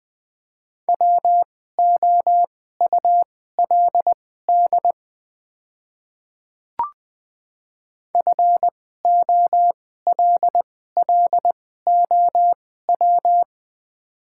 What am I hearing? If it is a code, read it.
Morse code, heard as a keyed tone, about 20 wpm, spaced as for 14 wpm, 700 Hz. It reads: WOULD FOLLOW
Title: 2x Character Spacing - Top 200 Words - 20wpm